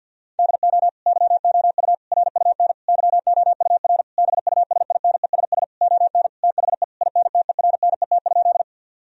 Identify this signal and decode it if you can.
Morse code, heard as a keyed tone, about 50 wpm, 700 Hz. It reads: DO YOU RUN YOUR BUSINESS ON THE INTERNET?
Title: Sentences from Top 1000 Words - Review Part 4 - 50wpm